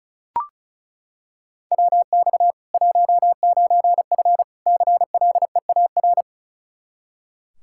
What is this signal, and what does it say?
Morse code, heard as a keyed tone, about 36 wpm, 700 Hz. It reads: WX 19F CLEAR
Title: Practice QSO - Exchange 2 - Info - 35wpm